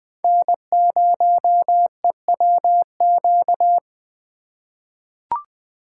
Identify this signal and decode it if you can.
Morse code, heard as a keyed tone, about 20 wpm, 700 Hz. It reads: N0EWQ
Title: USA Call Signs - 2x Character Spacing - 20wpm